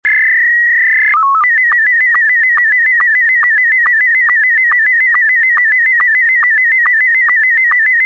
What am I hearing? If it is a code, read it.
SSTV (slow-scan television), Scottie 1.
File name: scottie s1